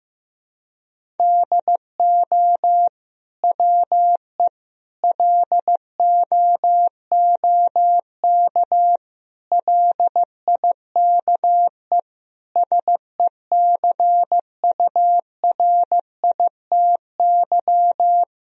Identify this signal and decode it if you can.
Morse code, heard as a keyed tone, about 16 wpm, 700 Hz. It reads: DO WE LOOK LIKE SECURITY
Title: Sentences from Top 1000 Words - Review Part 4 - 15wpm